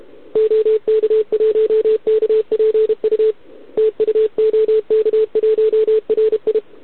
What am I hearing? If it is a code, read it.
Morse code, heard as a keyed tone, about 32 wpm, 440 Hz. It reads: OK1KPU TU OK1RI
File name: ARCHIV / OKDX2020 / OK1RIb